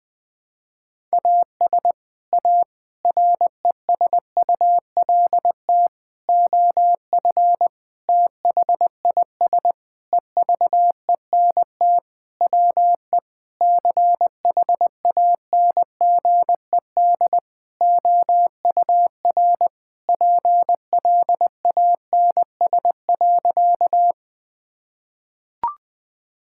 Morse code, heard as a keyed tone, about 20 wpm, 700 Hz. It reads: AS A RESULT OF THIS EVENT WE CHANGED OUR PLANS.
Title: Sentences with 5-Word Phrases - 20wpm